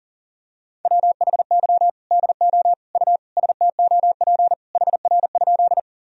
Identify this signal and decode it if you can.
Morse code, heard as a keyed tone, about 40 wpm, 700 Hz. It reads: WHY DO U STOP HR?
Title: Sentences from Top 300 Words and Abbreviations 40wpm